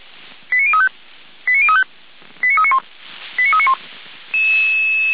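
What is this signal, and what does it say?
Five-tone selective call (ZVEI), heard as quick runs of five tones, tones of about 70 ms. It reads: 89035 89035 89391 89391
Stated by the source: ZVEI five-tone call 89502 received by SDR, from feuerwehr-orlamuende (fire brigade Orlamünde) server.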